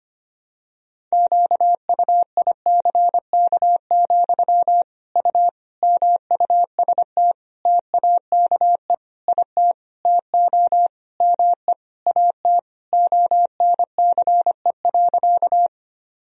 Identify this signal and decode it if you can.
Morse code, heard as a keyed tone, about 25 wpm, 700 Hz. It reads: QUICK, U MUST TAKE IT TO ME AT ONCE.
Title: Sentences from Top 500 Words and Abbreviations 25wpm